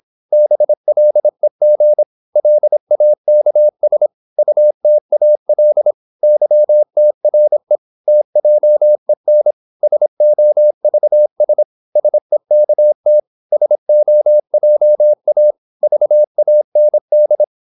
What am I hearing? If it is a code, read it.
Morse code, heard as a keyed tone, about 26 wpm, 600 Hz. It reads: BLEG LAKS UTAL YTRE TJEN SOVS SEKT SOJA VAND
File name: Korte ord DK 26wpm